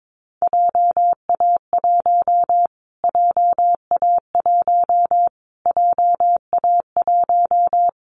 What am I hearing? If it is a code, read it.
Morse code, heard as a keyed tone, about 22 wpm, 700 Hz. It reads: JA1 JA1 JA1
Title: Callsign Prefixes for Top 25 DXCC - Repeated 3x - 22wpm